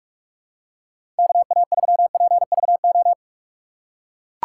Morse code, heard as a keyed tone, about 41 wpm, 700 Hz. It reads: KA3PVO
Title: USA Call Signs - 6x Character Spacing - 45wpm